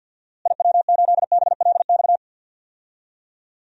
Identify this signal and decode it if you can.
Morse code, heard as a keyed tone, about 52 wpm, 700 Hz. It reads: IW8BLX
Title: DXCC-100 Call Signs - 6 Characters - Repeated 3x - 50wpm